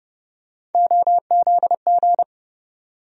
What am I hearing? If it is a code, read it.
Morse code, heard as a keyed tone, about 30 wpm, 700 Hz. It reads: OZG
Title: USA Call Sign Postfixes - 3 Letters 30wpm